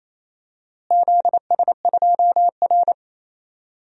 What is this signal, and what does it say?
Morse code, heard as a keyed tone, about 28 wpm, 700 Hz. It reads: ZS2R
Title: DXCC-100 Call Signs - 4 Characters - Repeated 3x - 28wpm